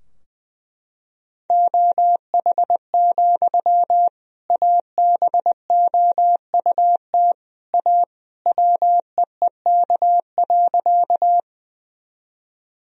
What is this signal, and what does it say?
Morse code, heard as a keyed tone, about 19 wpm, 700 Hz. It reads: OH, ABOUT A WEEK.
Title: Sentences from Top 1000 Words - Review Part 2 - 20wpm